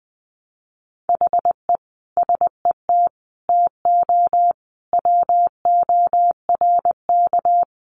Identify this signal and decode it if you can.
Morse code, heard as a keyed tone, about 20 wpm, 700 Hz. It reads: HE SET TO WORK
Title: Sentences from Top 1200 Words 20wpm